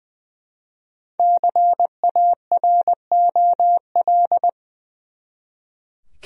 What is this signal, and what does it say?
Morse code, heard as a keyed tone, about 20 wpm, 700 Hz. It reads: CAROL
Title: USA Common Names - Max 5 Letters - 20wpm